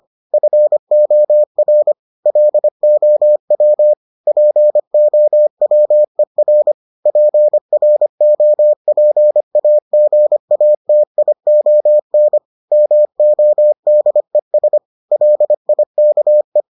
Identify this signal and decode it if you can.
Morse code, heard as a keyed tone, about 25 wpm, 600 Hz. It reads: FOR LOW POWER PROPAGATION MODES LIKE